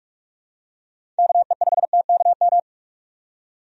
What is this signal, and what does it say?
Morse code, heard as a keyed tone, about 42 wpm, 700 Hz. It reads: KE5TKM
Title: USA Call Signs - 6 Characters - 45wpm